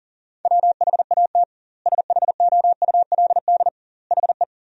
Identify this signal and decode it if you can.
Morse code, heard as a keyed tone, about 39 wpm, 700 Hz. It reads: WHAT SHOULD HE
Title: 2x Word-Spacing - Sentences from Top 300 Words - 40wpm